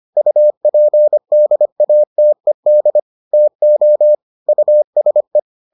Morse code, heard as a keyed tone, about 25 wpm, 600 Hz. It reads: UPDATED TO USE